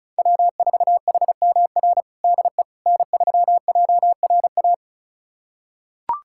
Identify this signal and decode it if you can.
Morse code, heard as a keyed tone, about 34 wpm, 700 Hz. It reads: W4HMR DE N3JRA
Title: USA Callsign DE Callsign - 5 Characters - 35wpm